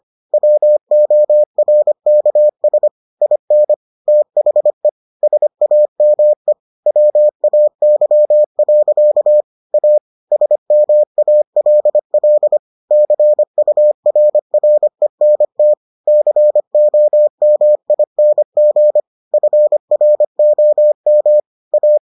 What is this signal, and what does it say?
Morse code, heard as a keyed tone, about 25 wpm, 600 Hz. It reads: WORKS IN THE SAME WAY. A SMALL CURRENT COMING FROM A